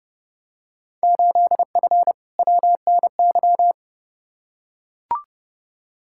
Morse code, heard as a keyed tone, about 30 wpm, 700 Hz. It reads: 8F WNY
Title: Field Day Exchanges Encore - 30wpm